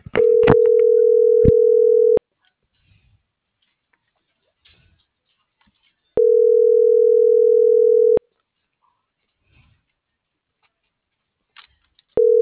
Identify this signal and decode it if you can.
Ringback tone (North America), 440 + 480 Hz.